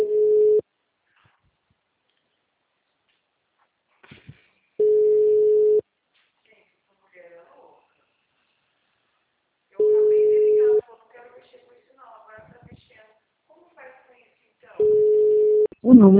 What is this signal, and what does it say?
Ringback tone (Europe, 425 Hz), with speech.